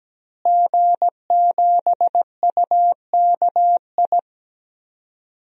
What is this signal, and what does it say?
Morse code, heard as a keyed tone, about 17 wpm, 700 Hz. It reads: G7UKI
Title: DXCC-100 Call Signs - 5 Characters - Repeated 3x - 17wpm